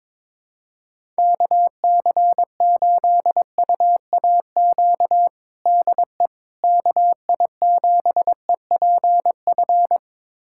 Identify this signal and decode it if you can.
Morse code, heard as a keyed tone, about 22 wpm, 700 Hz. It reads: KC8UAQ DE KI7EPF
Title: USA Callsign DE Callsign - 6 Characters - 22wpm